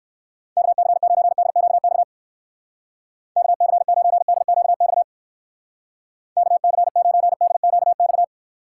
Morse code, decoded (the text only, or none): KC9DQX KC9DQX KC9DQX